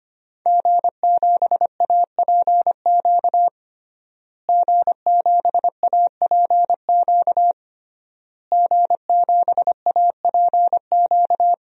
Morse code, heard as a keyed tone, about 25 wpm, 700 Hz. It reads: G7APQ G7APQ G7APQ
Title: DXCC-100 Call Signs - 5 Characters - Repeated 3x - 25wpm